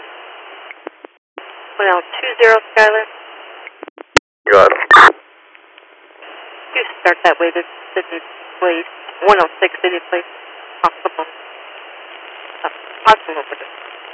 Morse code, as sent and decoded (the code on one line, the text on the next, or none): none